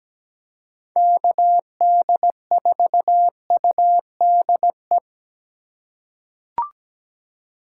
Morse code, heard as a keyed tone, about 17 wpm, 700 Hz. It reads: KD4UDE
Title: USA Call Signs - 6 Characters - 2x Character Spacing - 17wpm